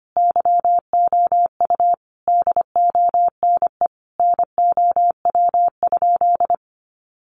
Morse code, heard as a keyed tone, about 25 wpm, 700 Hz. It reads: YOU DONE NOW?
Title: Sentences from Top 1100 Words 25wpm